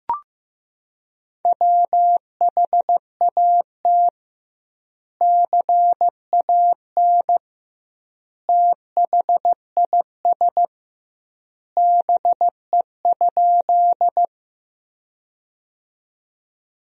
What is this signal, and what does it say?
Morse code, heard as a keyed tone, about 16 wpm, 700 Hz. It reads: WHAT CAN THIS BE?
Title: 2x Word-Spacing - Sentences from Top 100 Words - 15wpm